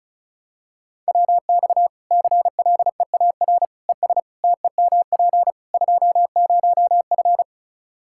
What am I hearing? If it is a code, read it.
Morse code, heard as a keyed tone, about 34 wpm, 700 Hz. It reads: WX CLEAR ES TEMP 20F
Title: Practice QSO - Exchange 2 - Info - 35wpm